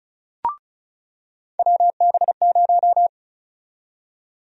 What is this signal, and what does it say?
Morse code, heard as a keyed tone, about 36 wpm, 700 Hz. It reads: WB0
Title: USA Call Sign Prefixes - 35wpm